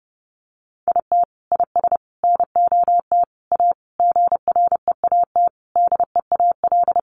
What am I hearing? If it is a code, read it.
Morse code, heard as a keyed tone, about 28 wpm, 700 Hz. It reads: IT IS NOT A GREAT DEAL
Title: Sentences from Top 800 Words 30wpm